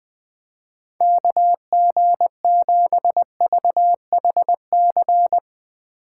Morse code, heard as a keyed tone, about 20 wpm, 700 Hz. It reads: KG7VHC
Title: Speed Racing - USA 2x3 Call Signs 17wpm